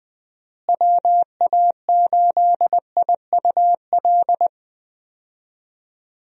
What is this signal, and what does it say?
Morse code, heard as a keyed tone, about 21 wpm, 700 Hz. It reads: WA8IUL
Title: USA Call Signs - 6 Characters - Repeated 3x - 20wpm